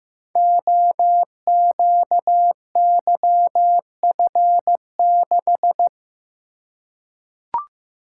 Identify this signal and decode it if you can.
Morse code, heard as a keyed tone, about 15 wpm, 700 Hz. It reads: OQYF6
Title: Code Groups - 5 Letter:Numbers - 15wpm